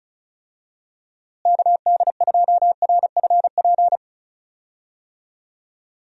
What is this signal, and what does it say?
Morse code, heard as a keyed tone, about 34 wpm, 700 Hz. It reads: KD2RFP
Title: USA Call Signs - 6 Characters - Repeated 3x - 35wpm